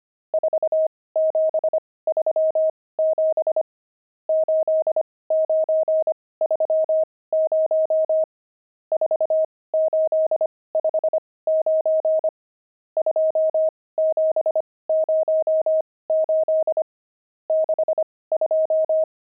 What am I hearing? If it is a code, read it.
Morse code, heard as a keyed tone, about 25 wpm, spaced as for 18 wpm, 630 Hz. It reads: 4737 8930 4859 2708 62
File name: pangram-fs-18-cs-25-nums12